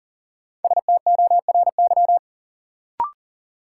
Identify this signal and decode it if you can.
Morse code, heard as a keyed tone, about 40 wpm, 700 Hz. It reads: STORY
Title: Sets of 3 Words - New Word Drill - 40wpm